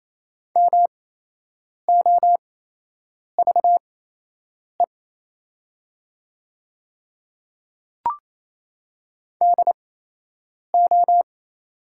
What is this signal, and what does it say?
Morse code, heard as a keyed tone, about 28 wpm, spaced as for 8 wpm, 700 Hz. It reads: MOVE DO
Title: 8x Character Spacing - Top 200 Words - 28wpm